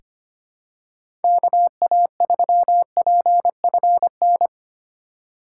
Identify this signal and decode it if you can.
Morse code, heard as a keyed tone, about 25 wpm, 700 Hz. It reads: KA3PFN
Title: USA Call Signs - 6 Characters - 3x Character Spacing - 25wpm